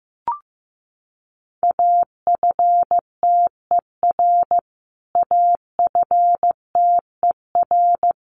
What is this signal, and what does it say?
Morse code, heard as a keyed tone, about 15 wpm, 700 Hz. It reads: AFTER AFTER
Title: Top 500 Words Repeated 15wpm